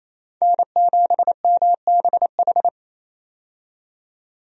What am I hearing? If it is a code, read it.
Morse code, heard as a keyed tone, about 28 wpm, 700 Hz. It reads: N7MBH
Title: USA Call Signs - 5 Characters - 28wpm